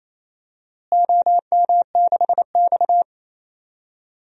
Morse code, heard as a keyed tone, about 28 wpm, 700 Hz. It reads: OM6X
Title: DXCC-100 Call Signs - 4 Characters - Repeated 3x - 28wpm